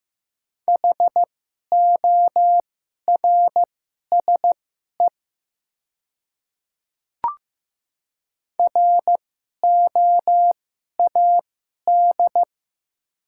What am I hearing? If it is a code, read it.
Morse code, heard as a keyed tone, about 15 wpm, spaced as for 11 wpm, 700 Hz. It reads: HORSE ROAD
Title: 2x Character Spacing - Top 500 Words - 15wpm